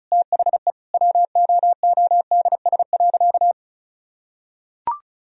Morse code, heard as a keyed tone, about 36 wpm, 700 Hz. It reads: THE WOODS.
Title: Sentences from Top 1000 Words - Review Part 2 - 35wpm